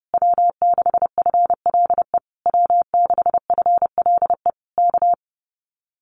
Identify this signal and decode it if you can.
Morse code, heard as a keyed tone, about 30 wpm, 700 Hz. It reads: W6FLE W6FLE K
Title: Common QSO Phrases Encore 30wpm